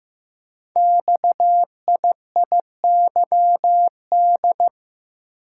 Morse code, heard as a keyed tone, about 15 wpm, 700 Hz. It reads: XIIYD
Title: Code Groups - 5 Letter:Numbers - 15wpm